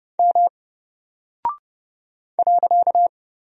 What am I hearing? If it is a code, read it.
Morse code, heard as a keyed tone, about 30 wpm, 700 Hz. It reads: M .
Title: Single Character (Letter:Number:Punctuation:Prosign) Alt 30wpm